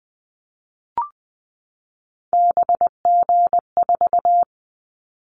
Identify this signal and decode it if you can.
Morse code, heard as a keyed tone, about 20 wpm, 700 Hz. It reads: BG4
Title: Callsign Prefixes for Top 100 DXCC - 20wpm